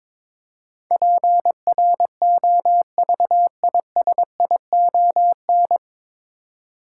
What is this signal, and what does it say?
Morse code, heard as a keyed tone, about 22 wpm, 700 Hz. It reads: PROVISION
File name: 9 Letter Words - First 3 Letters Given - 22wpm